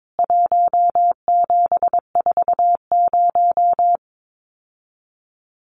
Morse code, heard as a keyed tone, about 22 wpm, 700 Hz. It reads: 1740
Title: CWT Exchange 22wpm